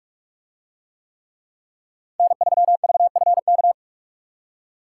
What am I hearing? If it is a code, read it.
Morse code, heard as a keyed tone, about 44 wpm, 700 Hz. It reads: N3VFK